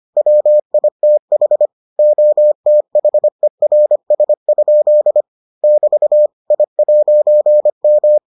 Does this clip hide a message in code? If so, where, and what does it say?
Yes: Morse code from 0.2 to 8.2 s, heard as a keyed tone, about 25 wpm, 600 Hz. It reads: WITH OTHERS? <BT> I'M